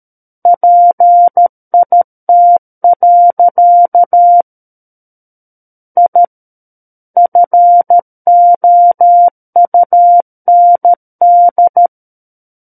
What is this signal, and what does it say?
Morse code, heard as a keyed tone, about 13 wpm, 700 Hz. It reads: PIT. I FOUND
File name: Book One / 04-Chapter 4 - 13 WPM